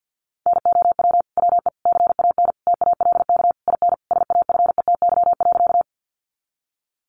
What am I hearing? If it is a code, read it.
Morse code, heard as a keyed tone, about 50 wpm, 700 Hz. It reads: NOW WE CAN TALK IN SAFETY.
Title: Sentences from Top 700 Words 50wpm